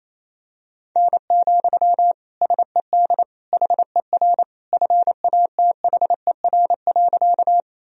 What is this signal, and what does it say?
Morse code, heard as a keyed tone, about 28 wpm, 700 Hz. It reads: N, SED HER FATHER.
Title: Sentences from Top 300 Words and Abbreviations 28wpm